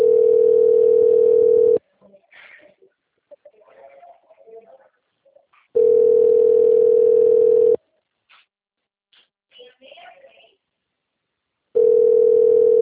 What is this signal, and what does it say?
Ringback tone (North America), 440 + 480 Hz, with speech.
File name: out- - - - - .3178